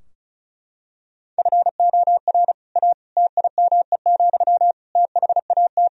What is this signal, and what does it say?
Morse code, heard as a keyed tone, about 34 wpm, 700 Hz. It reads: FOR A TIME, THAT
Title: Sentences from Top 1000 Words - Review Part 4 - 35wpm